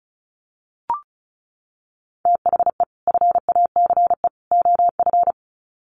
Morse code, heard as a keyed tone, about 34 wpm, 700 Hz. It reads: THE FACE OF
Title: Sets of 3 Words 35wpm